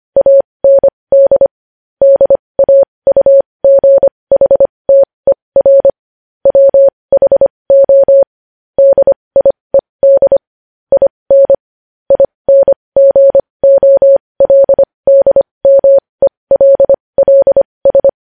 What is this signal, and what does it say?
Morse code, heard as a keyed tone, about 25 wpm, spaced as for 20 wpm, 550 Hz. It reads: AND DAUGHTER WHO DIED IN INGOLDMELLS
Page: news headlines in morse code at 20 wpm